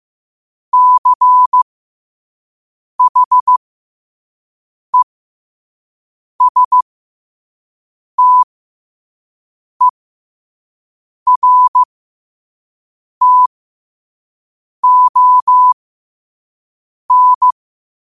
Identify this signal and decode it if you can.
Morse code, heard as a keyed tone, about 15 wpm, spaced as for 5 wpm, 1000 Hz. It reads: CHESTERTON